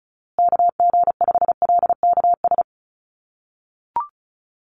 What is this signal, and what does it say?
Morse code, heard as a keyed tone, about 34 wpm, 700 Hz. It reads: KG5LKS
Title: USA Call Signs Repeated 3x 35wpm